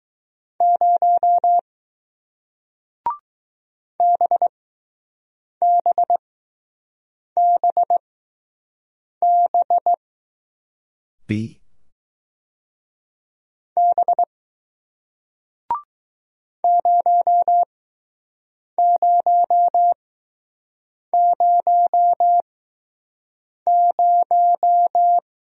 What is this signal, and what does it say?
Morse code, heard as a keyed tone, about 20 wpm, 700 Hz. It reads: 0 B B B B B 0 0 0 0